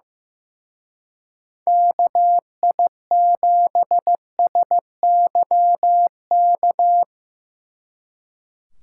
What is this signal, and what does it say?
Morse code, heard as a keyed tone, about 17 wpm, 700 Hz. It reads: KI7SYK
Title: Speed Racing - USA 2x3 Call Signs 15wpm